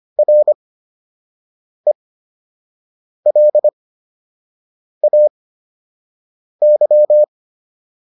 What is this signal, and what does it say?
Morse code, heard as a keyed tone, about 25 wpm, spaced as for 6 wpm, 600 Hz. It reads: RELAY